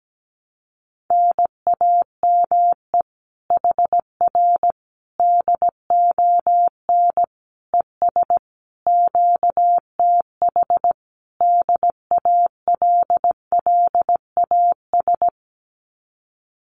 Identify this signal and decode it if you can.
Morse code, heard as a keyed tone, about 17 wpm, 700 Hz. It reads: NAME HR DON ES QTH DALLAS
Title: Common QSO Phrases Encore 17wpm